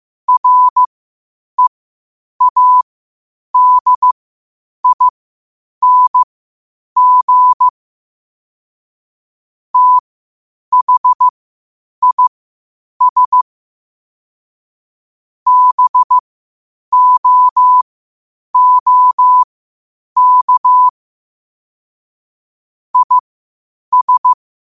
Morse code, heard as a keyed tone, about 15 wpm, spaced as for 8 wpm, 1000 Hz. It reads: READING THIS BOOK IS
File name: qotdfiles / 8wpm